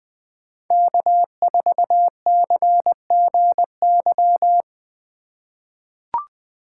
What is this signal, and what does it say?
Morse code, heard as a keyed tone, about 21 wpm, 700 Hz. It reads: K4CGY
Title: 2023 POTA Hunter Call Signs - 20wpm